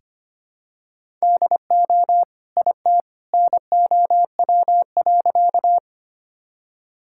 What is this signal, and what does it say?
Morse code, heard as a keyed tone, about 25 wpm, 700 Hz. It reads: DO IT NOW.
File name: practice / 25wpm / Sentences with 2-Word Phrases - 25wpm